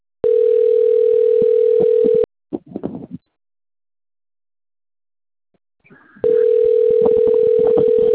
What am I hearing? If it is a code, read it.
Ringback tone (North America), 440 + 480 Hz.
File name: rg-1600-unknown- - - .35690